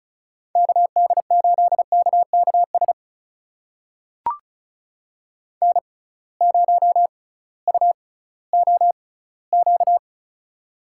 Morse code, heard as a keyed tone, about 36 wpm, spaced as for 19 wpm, 700 Hz. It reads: KD8KKS N0UOQ
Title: USA Call Signs - 6x Character Spacing - 35wpm